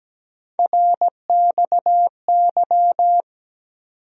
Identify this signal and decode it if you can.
Morse code, heard as a keyed tone, about 17 wpm, 700 Hz. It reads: RXY
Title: USA Call Sign Postfixes - 3 Letters 17wpm